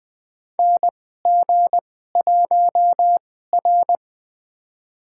Morse code, heard as a keyed tone, about 20 wpm, spaced as for 15 wpm, 700 Hz. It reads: NG1R
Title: USA Call Signs - 4 Characters - 2x Character Spacing - 20wpm